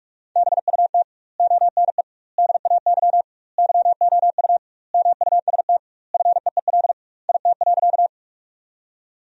Morse code, heard as a keyed tone, about 48 wpm, 700 Hz. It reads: BUT ONE DAY YOU MUST FEEL IT.
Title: 2x Word-Spacing - Sentences from Top 400 Words - 45wpm